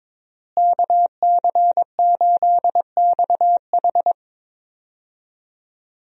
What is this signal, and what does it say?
Morse code, heard as a keyed tone, about 22 wpm, 700 Hz. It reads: KC8XH
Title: USA Call Signs - 5 Characters - Repeated 3x - 22wpm